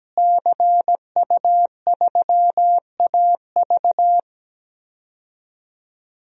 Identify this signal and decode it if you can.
Morse code, heard as a keyed tone, about 17 wpm, 700 Hz. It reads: CU3AV
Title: DXCC-100 Call Signs - 5 Characters - Repeated 2x - 17wpm